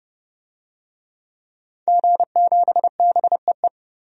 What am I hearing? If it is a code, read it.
Morse code, heard as a keyed tone, about 29 wpm, 700 Hz. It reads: G7BEE
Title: DXCC-100 Call Signs - 5 Characters - 30wpm